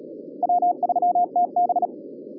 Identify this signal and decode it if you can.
Morse code, heard as a keyed tone, about 37 wpm, 720 Hz. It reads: W3TB